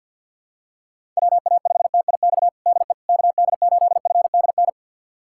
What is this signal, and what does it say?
Morse code, heard as a keyed tone, about 51 wpm, 700 Hz. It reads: WA5TIX DE KD8FDN
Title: USA Callsign DE Callsign - 6 Characters - 50wpm